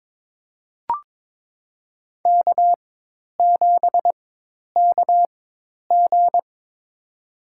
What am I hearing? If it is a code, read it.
Morse code, heard as a keyed tone, about 22 wpm, spaced as for 10 wpm, 700 Hz. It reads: K7KG
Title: USA Call Signs - 4 Characters - 4x Character Spacing - 22wpm